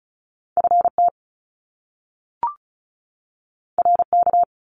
Morse code, heard as a keyed tone, about 36 wpm, 700 Hz. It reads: FT RK
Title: Top Two Letter Combinations in Words 35wpm